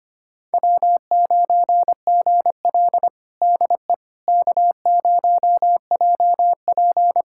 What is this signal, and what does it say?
Morse code, heard as a keyed tone, about 25 wpm, 700 Hz. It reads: W9GL DE K0JP
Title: Practice QSO - Exchange 4 - Ending - 25wpm